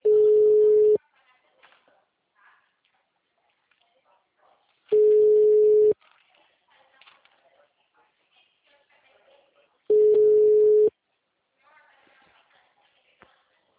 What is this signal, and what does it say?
Ringback tone (Europe, 425 Hz), with speech.
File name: out- -2006- - - .15276